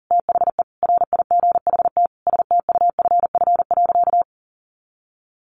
Morse code, heard as a keyed tone, about 40 wpm, 700 Hz. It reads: THE RIGHT STUFF.
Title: Sentences from Top 1400 Words 40wpm